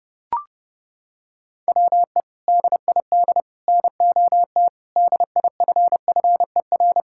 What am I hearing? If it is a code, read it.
Morse code, heard as a keyed tone, about 29 wpm, 700 Hz. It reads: WE DID NOT DIFFER
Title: Sentences from Top 1000 Words - Review Part 6 - 30wpm